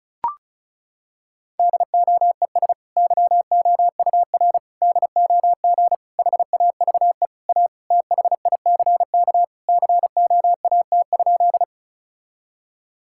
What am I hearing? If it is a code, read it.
Morse code, heard as a keyed tone, about 36 wpm, 700 Hz. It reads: DOES YOUR DOG HAVE A THICK COAT?
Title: Sentences from Top 1600 Words 35wpm